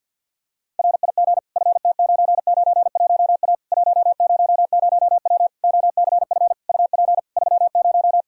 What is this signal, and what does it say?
Morse code, heard as a keyed tone, about 49 wpm, 700 Hz. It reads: RIG FT991A 100W OCF UP 30
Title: Practice QSO - Exchange 2 - Info - 50wpm